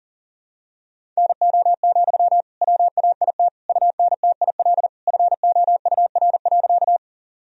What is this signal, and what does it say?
Morse code, heard as a keyed tone, about 40 wpm, 700 Hz. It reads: NO, WAIT UNTIL FOUR.
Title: Sentences from Top 1000 Words - Review Part 1 - 40wpm